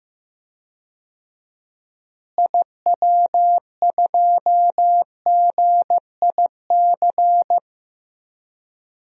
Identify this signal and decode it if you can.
Morse code, heard as a keyed tone, about 15 wpm, 700 Hz. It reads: IW2GIC
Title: DXCC-100 Call Signs - 6 Characters - Repeated 2x - 15wpm